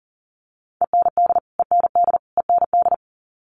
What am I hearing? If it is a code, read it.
Morse code, heard as a keyed tone, about 40 wpm, 700 Hz. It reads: END END END
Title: Top 300 Words Repeated 40wpm